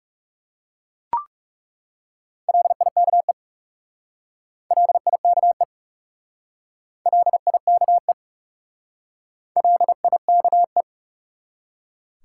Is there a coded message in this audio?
Yes: Morse code from 2.5 to 10.8 s, heard as a keyed tone, about 40 wpm, 700 Hz. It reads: LIKE LIKE LIKE LIKE